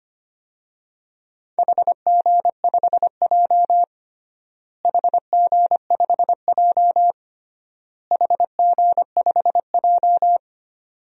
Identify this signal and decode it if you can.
Morse code, heard as a keyed tone, about 25 wpm, 700 Hz. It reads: HG5J HG5J HG5J
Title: DXCC-100 Call Signs - 4 Characters - Repeated 3x - 25wpm